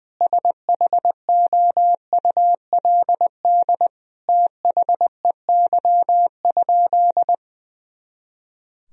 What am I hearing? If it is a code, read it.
Morse code, heard as a keyed tone, about 20 wpm, 700 Hz. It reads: SHOULD THEY?
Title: Sentences from Top 300 Words and Abbreviations 20wpm